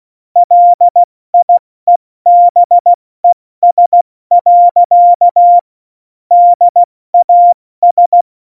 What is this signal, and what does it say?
Morse code, heard as a keyed tone, about 16 wpm, 700 Hz. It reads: LIEBES. DAS
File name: Morsekurs / Witze / Witz 25. Juni 2019, 70 F80